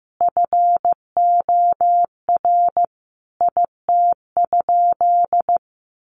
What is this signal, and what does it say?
Morse code, heard as a keyed tone, about 15 wpm, 700 Hz. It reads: FOR IT?
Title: Sentences from Top 900 Words 15wpm